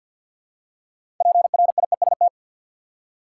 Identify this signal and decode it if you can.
Morse code, heard as a keyed tone, about 53 wpm, 700 Hz. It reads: PRIEST
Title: Six Letter Words - 50wpm